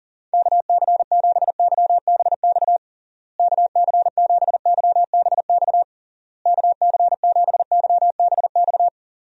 Morse code, heard as a keyed tone, about 40 wpm, 700 Hz. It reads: KC7YBX KC7YBX KC7YBX
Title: USA Call Signs - 6 Characters - Repeated 3x - 40wpm